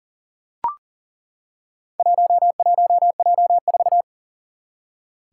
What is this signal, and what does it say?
Morse code, heard as a keyed tone, about 38 wpm, 700 Hz. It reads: 11J4
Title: Four Challenging Letter-Number Combinations 40wpm